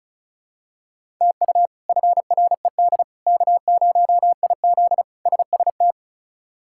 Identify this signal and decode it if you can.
Morse code, heard as a keyed tone, about 36 wpm, 700 Hz. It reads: TU FRED K0IZ SST
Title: K1USN SST QSO Practice - 35wpm